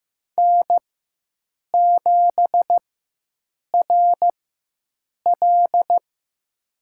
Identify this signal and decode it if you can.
Morse code, heard as a keyed tone, about 16 wpm, spaced as for 7 wpm, 700 Hz. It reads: N7RL
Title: USA Call Signs - 4x Character Spacing - 15wpm